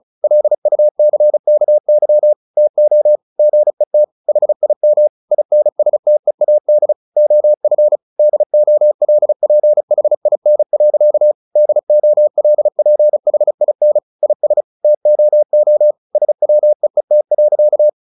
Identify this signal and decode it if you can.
Morse code, heard as a keyed tone, about 34 wpm, 600 Hz. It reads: LUCKY TO GET HIM INSTEAD OF DOLPHIN. DOLPHIN IS TOO SWEET.
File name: the old man and the sea / the old man and the sea 1 35wpm